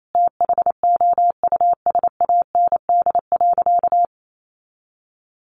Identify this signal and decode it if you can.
Morse code, heard as a keyed tone, about 28 wpm, 700 Hz. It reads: THOUSAND.
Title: Sentences from Top 1300 Words 28wpm